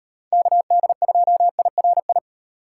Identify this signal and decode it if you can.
Morse code, heard as a keyed tone, about 38 wpm, 700 Hz. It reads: KD2IRI